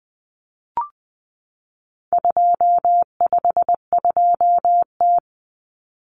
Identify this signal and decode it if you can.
Morse code, heard as a keyed tone, about 20 wpm, 700 Hz. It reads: 252T
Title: Common Cut Numbers - 4-Digit 20wpm